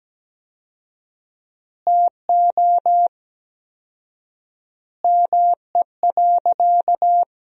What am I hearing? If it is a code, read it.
Morse code, heard as a keyed tone, about 17 wpm, 700 Hz. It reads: TO ME.
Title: 4x Word-Spacing - Sentences from Top 400 Words - 17wpm